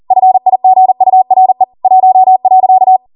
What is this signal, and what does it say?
Morse code, heard as a keyed tone, about 40 wpm, 750 Hz. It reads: FIGURE 1.